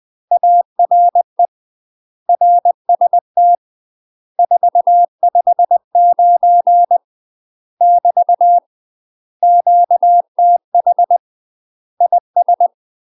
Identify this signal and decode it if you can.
Morse code, heard as a keyed tone, about 20 wpm, 700 Hz. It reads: ARE RST 459 <BT> QTH IS
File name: UK MASTER QSOs TRAINING / 20WPM / UK MASTER QSOs 0067